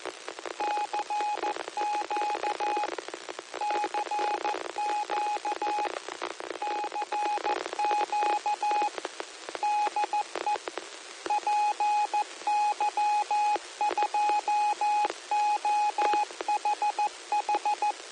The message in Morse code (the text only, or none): CQ CQ CQ DE PY2OHH